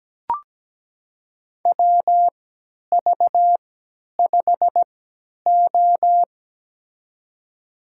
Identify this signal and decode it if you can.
Morse code, heard as a keyed tone, about 17 wpm, spaced as for 10 wpm, 700 Hz. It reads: WV5O
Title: USA Call Signs - 4 Characters - 3x Character Spacing - 17wpm